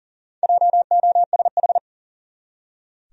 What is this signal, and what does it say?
Morse code, heard as a keyed tone, about 37 wpm, 700 Hz. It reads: JOSH
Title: USA Common Names - 40wpm